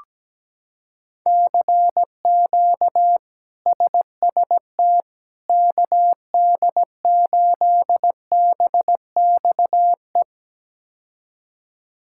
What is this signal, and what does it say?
Morse code, heard as a keyed tone, about 17 wpm, 700 Hz. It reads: CQ SST KD8BXE
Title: K1USN SST QSO Practice - 17wpm